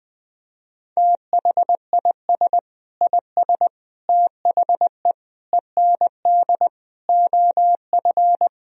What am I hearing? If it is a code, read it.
Morse code, heard as a keyed tone, about 20 wpm, 700 Hz. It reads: THIS IS THE END OF
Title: Sentences from Top 1000 Words - Review Part 1 - 20wpm